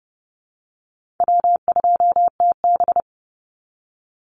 Morse code, heard as a keyed tone, about 30 wpm, 700 Hz. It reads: W2TB
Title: Call Signs Repeated 3x - 30wpm